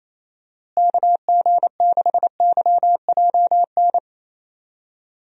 Morse code, heard as a keyed tone, about 28 wpm, 700 Hz. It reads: KG6YJN
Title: USA Call Signs - 6 Characters - Repeated 2x - 28wpm